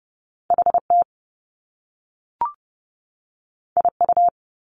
Morse code, heard as a keyed tone, about 30 wpm, 700 Hz. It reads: HT IU
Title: Top Two Letter Combinations in Words 30wpm